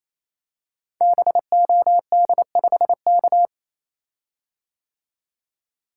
Morse code, heard as a keyed tone, about 28 wpm, 700 Hz. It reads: BOD5K
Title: Code Groups - 5 Letter:Numbers - 28wpm